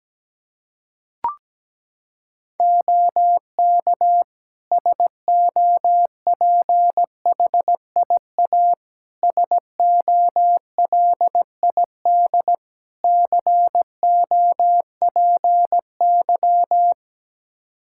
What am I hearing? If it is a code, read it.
Morse code, heard as a keyed tone, about 17 wpm, 700 Hz. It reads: OK SOPHIA SOLID COPY
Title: Practice QSO - Exchange 3 - More Info - 17wpm